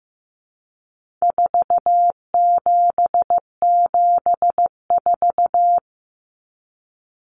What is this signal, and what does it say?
Morse code, heard as a keyed tone, about 14 wpm, 700 Hz. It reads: 4774
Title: Common Cut Numbers - 4-Digit 15wpm